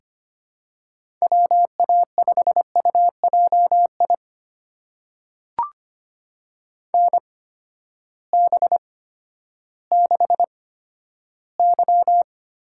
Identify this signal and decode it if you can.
Morse code, heard as a keyed tone, about 25 wpm, spaced as for 12 wpm, 700 Hz. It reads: WA5UJI NB6Y
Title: USA Call Signs - 8x Character Spacing - 25wpm